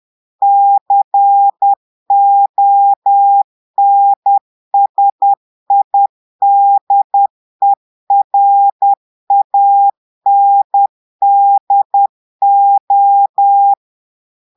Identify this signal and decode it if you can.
Morse code, heard as a keyed tone, about 10 wpm, 800 Hz. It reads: CONSIDERANDO
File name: portugais texte en clair 2 vitesse 10 mots minutes